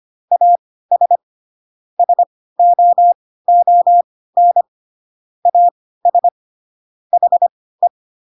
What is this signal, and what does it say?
Morse code, heard as a keyed tone, about 25 wpm, spaced as for 16 wpm, 700 Hz. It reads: AS SOON AS HE
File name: Pinocchio / Pinocchio ch1 10 25 ik1hge